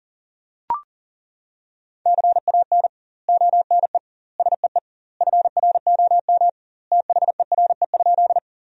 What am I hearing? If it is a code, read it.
Morse code, heard as a keyed tone, about 40 wpm, 700 Hz. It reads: CAN ONE SEE FROM THERE?
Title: 2x Word-Spacing - Sentences from Top 100 Words - 40wpm